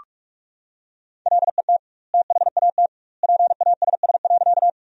Morse code, heard as a keyed tone, about 41 wpm, 700 Hz. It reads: LET THAT PASS.
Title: 2x Word-Spacing - Sentences from Top 400 Words - 45wpm